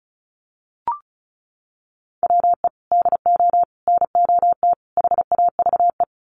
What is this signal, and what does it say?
Morse code, heard as a keyed tone, about 34 wpm, 700 Hz. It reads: WE DO NOT HAVE
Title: Sets of 4 Words 35wpm